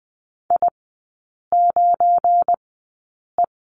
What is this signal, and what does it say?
Morse code, heard as a keyed tone, about 20 wpm, spaced as for 8 wpm, 700 Hz. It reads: I9E
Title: 178 Lesson 29e - Slash - Three Characters 20wpm